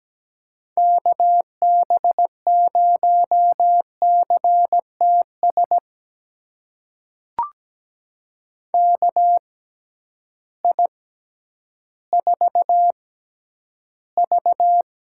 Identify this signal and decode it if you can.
Morse code, heard as a keyed tone, about 17 wpm, spaced as for 10 wpm, 700 Hz. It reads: KB0CTS KI4V
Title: USA Call Signs - 6 Characters - 6x Character Spacing - 17wpm